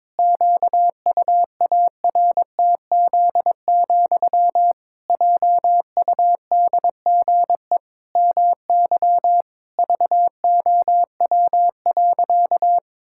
Morse code, heard as a keyed tone, about 22 wpm, 700 Hz. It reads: QUARTZ, JUDGE MY VOW.
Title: Pangram Challenge v2 - 22wpm